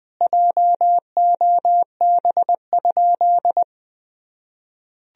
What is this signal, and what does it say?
Morse code, heard as a keyed tone, about 20 wpm, 700 Hz. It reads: JOB?
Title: Sentences from Top 1000 Words - Review Part 3 - 20wpm